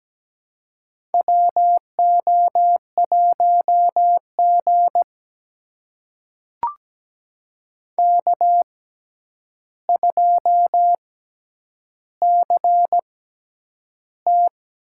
Morse code, heard as a keyed tone, about 17 wpm, spaced as for 9 wpm, 700 Hz. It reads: WO1G K2CT
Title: USA Call Signs - 6x Character Spacing - 17wpm